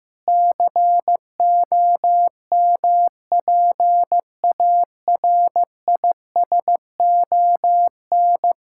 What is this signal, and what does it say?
Morse code, heard as a keyed tone, about 15 wpm, 700 Hz. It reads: COMPARISON
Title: Sentences from Top 1600 Words 15wpm